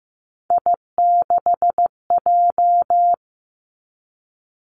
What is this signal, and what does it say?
Morse code, heard as a keyed tone, about 15 wpm, 700 Hz. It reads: I6J
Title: Three Challenging Letter-Number Combinations 15wpm